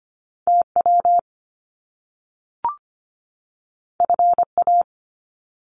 Morse code, heard as a keyed tone, about 25 wpm, 700 Hz. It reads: TW FA
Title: Top Two Letter Combinations in Words 25wpm